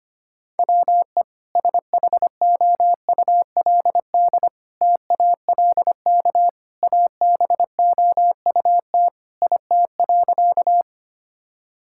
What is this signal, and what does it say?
Morse code, heard as a keyed tone, about 25 wpm, 700 Hz. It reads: WE SHOULD TALK ABOUT IT.
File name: practice / 25wpm / Sentences with 2-Word Phrases - 25wpm